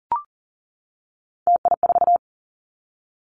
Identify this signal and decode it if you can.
Morse code, heard as a keyed tone, about 40 wpm, 700 Hz. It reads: TI4
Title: Callsign Prefixes for Top 100 DXCC - 40wpm